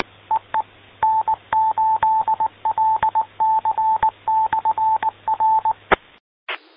Morse code, heard as a keyed tone, about 19 wpm, 880 Hz. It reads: EE N8LC/R